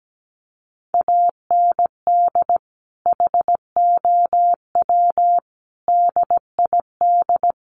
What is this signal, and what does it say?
Morse code, heard as a keyed tone, about 17 wpm, 700 Hz. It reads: AND HOW DID
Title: Sentences from Top 100 Words 17wpm